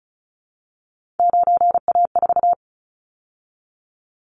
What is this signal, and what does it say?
Morse code, heard as a keyed tone, about 34 wpm, 700 Hz. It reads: 9A4
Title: Callsign Prefixes for Top 50 DXCC - 35wpm